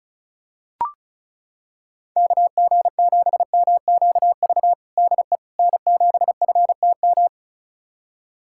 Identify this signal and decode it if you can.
Morse code, heard as a keyed tone, about 34 wpm, 700 Hz. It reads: KG7MQV DE N7FTM
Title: USA Callsign DE Callsign 35wpm